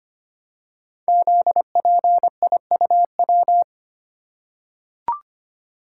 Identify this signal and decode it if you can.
Morse code, heard as a keyed tone, about 25 wpm, 700 Hz. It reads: ZPIUW
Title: Code Groups - 5 Letters - 25wpm